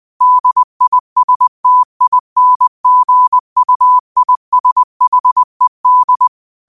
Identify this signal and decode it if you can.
Morse code, heard as a keyed tone, about 20 wpm, 1000 Hz. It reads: DISTINGUISHED
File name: qotdfiles / 20wpm